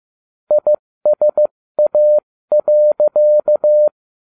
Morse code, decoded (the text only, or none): ISA.